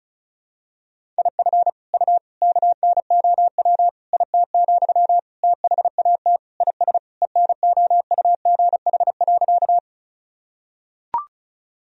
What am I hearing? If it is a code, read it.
Morse code, heard as a keyed tone, about 34 wpm, 700 Hz. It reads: IF U KNOW IT, THAT IS ENOUGH.
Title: Sentences from Top 400 Words and Abbreviations 35wpm